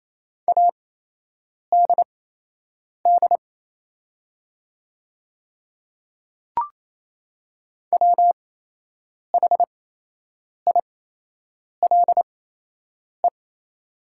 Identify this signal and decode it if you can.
Morse code, heard as a keyed tone, about 27 wpm, spaced as for 8 wpm, 690 Hz. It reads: ADD WHILE
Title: 8x Character Spacing - Top 500 Words - 28wpm